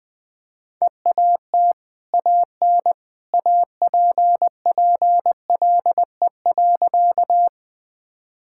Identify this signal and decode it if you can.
Morse code, heard as a keyed tone, about 21 wpm, 700 Hz. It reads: EAT AN APPLE.
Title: Easy Sentences with Hints - 20wpm